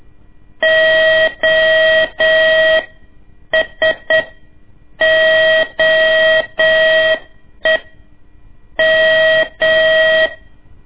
Morse code, heard as a keyed tone, about 8 wpm, spaced as for 6 wpm, 640 Hz, with noise. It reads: OSOEM
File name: ik1weg2 / CORSI CW / lez2-1